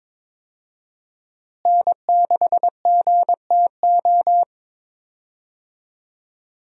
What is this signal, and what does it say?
Morse code, heard as a keyed tone, about 22 wpm, 700 Hz. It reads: N6GTO
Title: USA Call Signs - 5 Characters - Repeated 3x - 22wpm